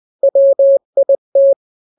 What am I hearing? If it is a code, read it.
Morse code, heard as a keyed tone, about 20 wpm, 550 Hz. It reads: WIT
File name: n-grams / 3-grams 20 18 WPM